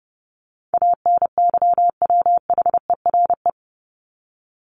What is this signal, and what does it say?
Morse code, heard as a keyed tone, about 30 wpm, 700 Hz. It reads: ANYWHERE
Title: Top 1251-1500 Words 30wpm